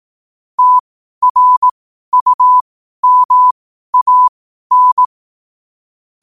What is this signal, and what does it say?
Morse code, heard as a keyed tone, about 18 wpm, spaced as for 12 wpm, 1000 Hz. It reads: TRUMAN
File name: qotdfiles / 12wpm